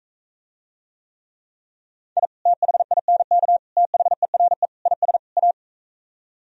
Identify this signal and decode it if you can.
Morse code, heard as a keyed tone, about 45 wpm, 700 Hz. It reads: I THINK THERE IS A